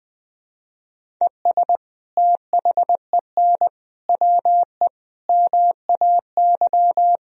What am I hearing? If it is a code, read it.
Morse code, heard as a keyed tone, about 21 wpm, 700 Hz. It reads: ES THEN WE MAY
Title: Sentences from Top 300 Words and Abbreviations 20wpm